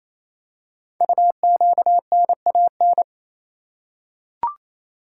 Morse code, heard as a keyed tone, about 28 wpm, 700 Hz. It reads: UQNAN
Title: Code Groups - 5 Letters - 28wpm